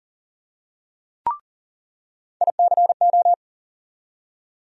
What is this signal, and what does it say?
Morse code, heard as a keyed tone, about 40 wpm, 700 Hz. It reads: ICO
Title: USA Call Sign Postfixes - 3 Letters 40wpm